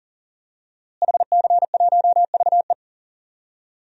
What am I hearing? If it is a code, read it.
Morse code, heard as a keyed tone, about 40 wpm, 700 Hz. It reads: HC1VE
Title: DXCC-100 Call Signs - 5 Characters - Repeated 2x - 40wpm